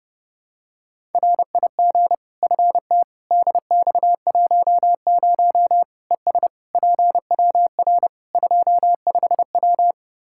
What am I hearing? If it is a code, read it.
Morse code, heard as a keyed tone, about 30 wpm, 700 Hz. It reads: RIG FT DX10 ES PWR 25W
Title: Practice QSO - Exchange 2 - Info - 30wpm